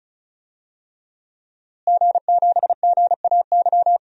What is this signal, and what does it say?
Morse code, heard as a keyed tone, about 34 wpm, 700 Hz. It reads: G7GAY
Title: DXCC-100 Call Signs - 5 Characters - Repeated 3x - 35wpm